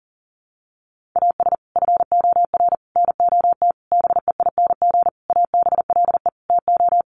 Morse code, heard as a keyed tone, about 40 wpm, 700 Hz. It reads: AS FOR NOT BEING ABLE TO
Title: Sentences from Top 1100 Words 40wpm